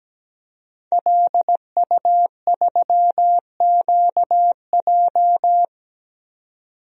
Morse code, heard as a keyed tone, about 17 wpm, 700 Hz. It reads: LU3QJ
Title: DXCC-100 Call Signs - 5 Characters - 17wpm